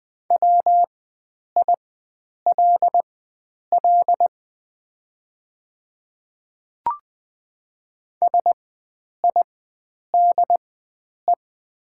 Morse code, heard as a keyed tone, about 20 wpm, spaced as for 9 wpm, 700 Hz. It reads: WILL SIDE
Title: 4x Character Spacing - Top 100 Words - 20wpm